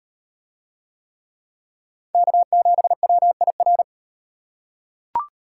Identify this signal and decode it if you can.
Morse code, heard as a keyed tone, about 38 wpm, 700 Hz. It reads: K7WIR